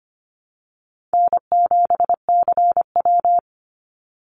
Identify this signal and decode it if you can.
Morse code, heard as a keyed tone, about 25 wpm, 700 Hz. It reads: N7CW
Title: NA SOTA Activator Callsigns 25wpm